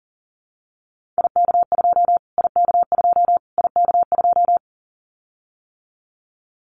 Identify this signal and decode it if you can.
Morse code, heard as a keyed tone, about 40 wpm, 700 Hz. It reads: IK2 IK2 IK2
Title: Callsign Prefixes for Top 25 DXCC - Repeated 3x - 40wpm